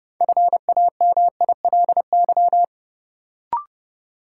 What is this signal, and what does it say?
Morse code, heard as a keyed tone, about 30 wpm, 700 Hz. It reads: FAMILY
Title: Sentences from Top 400 Words - New Word Drill - 30wpm